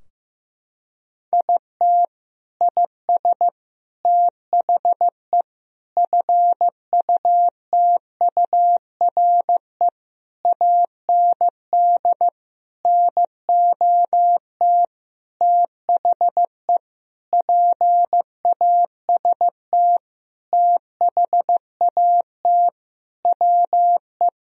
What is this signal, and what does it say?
Morse code, heard as a keyed tone, about 15 wpm, 700 Hz. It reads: IT IS THE FUTURE AND NOT THE PAST THAT WE
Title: Sentences from Top 1000 Words - Review Part 4 - 15wpm